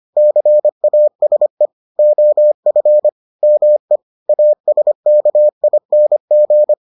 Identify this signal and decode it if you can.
Morse code, heard as a keyed tone, about 25 wpm, 600 Hz. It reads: CASE OF ME ASKING